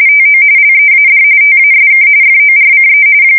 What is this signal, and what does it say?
RTTY, about 45.45 baud. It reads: TO LEAD TASK FORCE
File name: fsk170 / battleshipnj